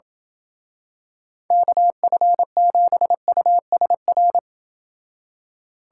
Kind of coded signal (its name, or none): Morse code